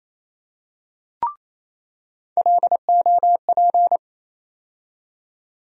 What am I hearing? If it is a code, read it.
Morse code, heard as a keyed tone, about 27 wpm, 700 Hz. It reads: LOP
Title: USA Call Sign Postfixes - 3 Letters 28wpm